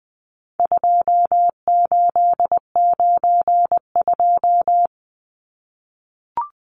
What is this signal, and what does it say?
Morse code, heard as a keyed tone, about 20 wpm, 700 Hz. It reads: 2892
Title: Numbers - 4-Digit 20wpm